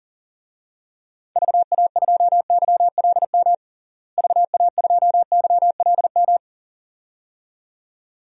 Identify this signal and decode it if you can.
Morse code, heard as a keyed tone, about 40 wpm, 700 Hz. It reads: VA2YLM VA2YLM
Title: DXCC-100 Call Signs - 6 Characters - Repeated 2x - 40wpm